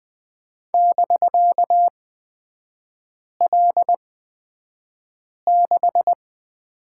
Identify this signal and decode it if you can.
Morse code, heard as a keyed tone, about 20 wpm, spaced as for 5 wpm, 700 Hz. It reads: <BK>L6
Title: 250 Lesson 40e - BK - Three Characters 20wpm